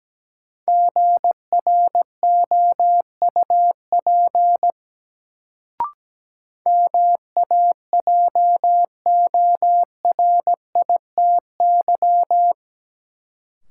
Morse code, heard as a keyed tone, about 17 wpm, 700 Hz. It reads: GROUP MAJORITY
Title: Sets of 3 Words - New Word Drill - 17wpm